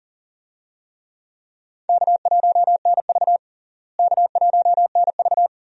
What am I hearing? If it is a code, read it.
Morse code, heard as a keyed tone, about 40 wpm, 700 Hz. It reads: K1NV K1NV
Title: USA Call Signs - 4 Characters - Repeated 3x - 40wpm